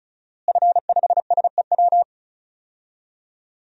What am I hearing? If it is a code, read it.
Morse code, heard as a keyed tone, about 36 wpm, 700 Hz. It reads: F5SEW
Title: DXCC-100 Call Signs - 5 Characters - Repeated 2x - 35wpm